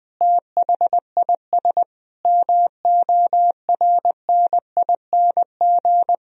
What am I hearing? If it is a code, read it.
Morse code, heard as a keyed tone, about 21 wpm, 700 Hz. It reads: THIS MORNING
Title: Sentences from Top 1000 Words - Review Part 4 - 20wpm